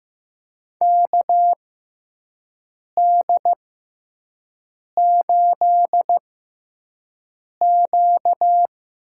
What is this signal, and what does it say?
Morse code, heard as a keyed tone, about 15 wpm, spaced as for 5 wpm, 700 Hz. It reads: KD8Q
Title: USA Call Signs - 6x Character Spacing - 15wpm